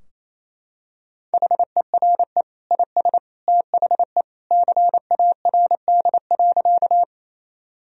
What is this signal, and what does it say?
Morse code, heard as a keyed tone, about 28 wpm, 700 Hz. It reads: HERE IS THE CARD.
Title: Sentences from Top 1000 Words - Review Part 4 - 28wpm